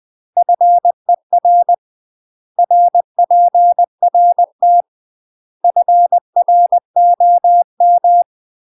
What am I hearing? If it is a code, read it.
Morse code, heard as a keyed tone, about 20 wpm, 700 Hz. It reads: FER RPRT FROM
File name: UK MASTER QSOs TRAINING / 20WPM / UK MASTER QSOs 0048